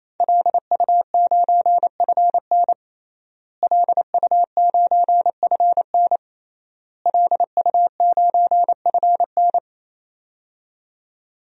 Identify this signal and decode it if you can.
Morse code, heard as a keyed tone, about 28 wpm, 700 Hz. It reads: LU9FN LU9FN LU9FN
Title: DXCC-100 Call Signs - 5 Characters - Repeated 3x - 28wpm